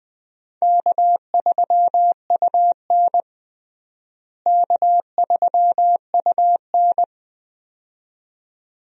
Morse code, heard as a keyed tone, about 21 wpm, 700 Hz. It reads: K3UN K3UN
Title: USA Call Signs - 4 Characters - Repeated 2x - 20wpm